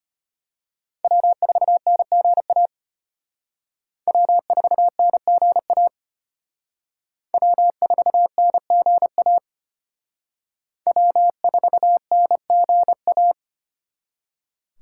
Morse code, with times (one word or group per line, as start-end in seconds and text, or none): 1.0-2.7 W4NGA
4.1-5.9 W4NGA
7.3-9.4 W4NGA
10.9-13.3 W4NGA